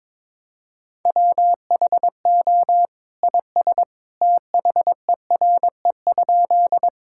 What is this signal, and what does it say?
Morse code, heard as a keyed tone, about 22 wpm, 700 Hz. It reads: WHO IS THERE?
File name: practice / 22wpm / Sentences with 2-Word Phrases - 22wpm